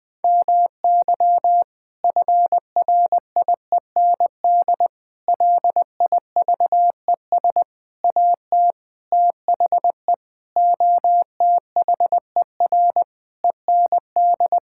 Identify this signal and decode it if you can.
Morse code, heard as a keyed tone, about 21 wpm, 700 Hz. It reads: MY FRIEND LIVES AT THE OTHER END
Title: Sentences with 5-Word Phrases - 20wpm